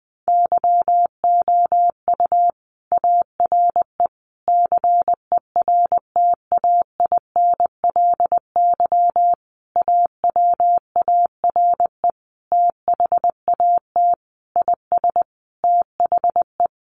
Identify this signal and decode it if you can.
Morse code, heard as a keyed tone, about 20 wpm, 700 Hz. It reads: YOU ARE CERTAINLY AWARE THAT IS THE
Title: Sentences from Top 1200 Words 20wpm